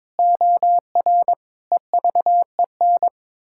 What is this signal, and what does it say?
Morse code, heard as a keyed tone, about 22 wpm, 700 Hz. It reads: OR EVEN
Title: Pangram Challenge v2 - 22wpm